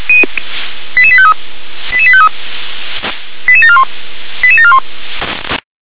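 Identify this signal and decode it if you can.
Five-tone selective call (ZVEI), heard as quick runs of five tones, tones of about 70 ms. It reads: 88853 88853 88741 88741
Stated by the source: ZVEI five-tone selective call received on SDR.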